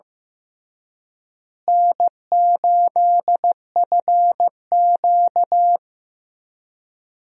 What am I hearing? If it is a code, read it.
Morse code, heard as a keyed tone, about 15 wpm, 700 Hz. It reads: N8FQ
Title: USA Call Signs - 4 Characters - Repeated 3x - 15wpm